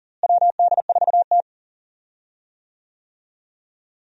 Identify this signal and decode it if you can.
Morse code, heard as a keyed tone, about 40 wpm, 700 Hz. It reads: WD4T
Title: USA Call Signs - 4 Characters - Repeated 3x - 40wpm